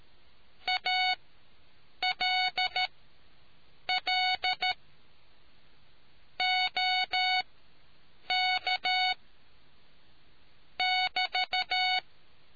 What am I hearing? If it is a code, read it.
Morse code, heard as a keyed tone, about 13 wpm, spaced as for 7 wpm, 740 Hz. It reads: ALL OK <BT>